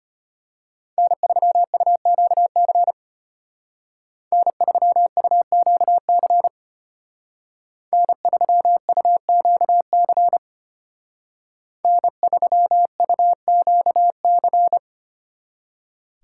Morse code, heard as a keyed tone, about 30 wpm, 700 Hz. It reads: N3UQC N3UQC N3UQC N3UQC
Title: Speed Racing - USA 1x3 Call Signs 25wpm